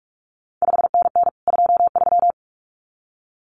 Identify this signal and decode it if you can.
Morse code, heard as a keyed tone, about 46 wpm, 700 Hz. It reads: 5NN 23
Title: CQ WW DX Contest Exchange Encore - 45wpm